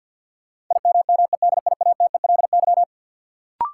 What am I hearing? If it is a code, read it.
Morse code, heard as a keyed tone, about 53 wpm, 700 Hz. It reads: IMMEDIATELY
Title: Sentences from Top 1100 Words - New Word Drill - 50wpm